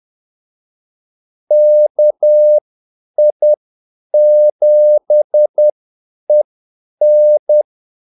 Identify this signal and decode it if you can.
Morse code, heard as a keyed tone, about 10 wpm, spaced as for 8 wpm, 600 Hz. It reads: KI7EN
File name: Callsigns-10-8wpm